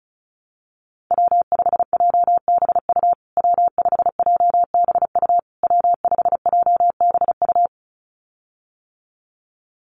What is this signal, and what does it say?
Morse code, heard as a keyed tone, about 34 wpm, 700 Hz. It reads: W5JBU W5JBU W5JBU
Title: USA Call Signs Repeated 3x 35wpm